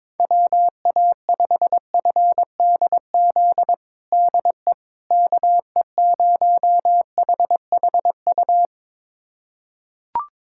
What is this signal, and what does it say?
Morse code, heard as a keyed tone, about 22 wpm, 700 Hz. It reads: WA5FDZ DE KE0HHU
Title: USA Callsign DE Callsign 22wpm